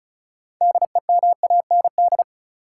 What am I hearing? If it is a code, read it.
Morse code, heard as a keyed tone, about 36 wpm, 700 Hz. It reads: DEMAND
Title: Sentences from Top 800 Words - New Word Drill - 35wpm